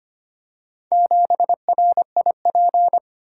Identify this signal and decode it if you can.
Morse code, heard as a keyed tone, about 25 wpm, 700 Hz. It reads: 7RIP
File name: practice / 25wpm / Four Letter-Number Combinations - 25wpm